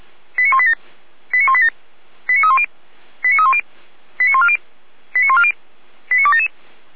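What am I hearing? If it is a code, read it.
Five-tone selective call (ZVEI), heard as quick runs of five tones, tones of about 70 ms. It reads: 89187 89187 89310 89310 89140 89140 89277
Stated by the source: ZVEI five-tone call 89229, received by SDR.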